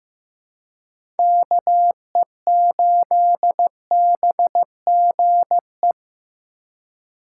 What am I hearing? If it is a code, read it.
Morse code, heard as a keyed tone, about 15 wpm, 700 Hz. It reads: KE8BGE
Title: USA Call Signs - 6 Characters - 2x Character Spacing - 15wpm